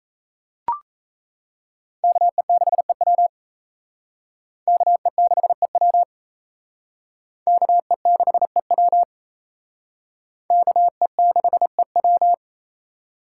Morse code, heard as a keyed tone, about 35 wpm, 700 Hz. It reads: KE6EW KE6EW KE6EW KE6EW
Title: Speed Racing - USA 2x2 Call Signs 28wpm